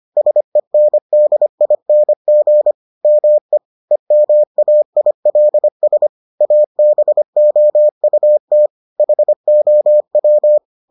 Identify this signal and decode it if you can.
Morse code, heard as a keyed tone, about 25 wpm, 600 Hz. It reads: SENDING ME EMAILS ABOUT HOW